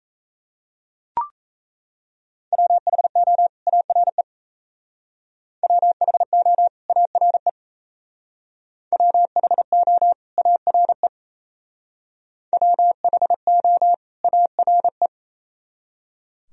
Morse code, heard as a keyed tone, about 32 wpm, 700 Hz. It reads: WHO ARE WHO ARE WHO ARE WHO ARE
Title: Speed-Racing - Sets of 2 Words 28wpm